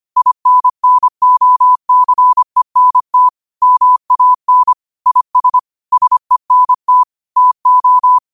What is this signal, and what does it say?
Morse code, heard as a keyed tone, about 25 wpm, 1000 Hz. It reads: INNOCENT MAN IS SENT TO
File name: qotdfiles / 25wpm